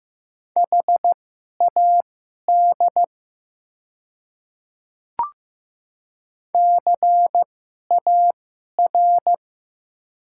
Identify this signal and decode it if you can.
Morse code, heard as a keyed tone, about 15 wpm, spaced as for 11 wpm, 700 Hz. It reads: HAD CAR
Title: 2x Character Spacing - Top 300 Words - 15wpm